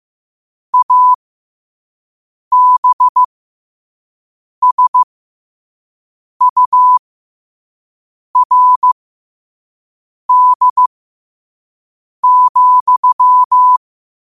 Morse code, heard as a keyed tone, about 15 wpm, spaced as for 5 wpm, 1000 Hz. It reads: ABSURD,
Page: New quotes every day in morse code at 5 Words per minute.